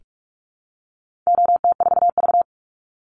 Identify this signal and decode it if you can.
Morse code, heard as a keyed tone, about 44 wpm, 700 Hz. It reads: OT4V
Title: Call Signs - 45wpm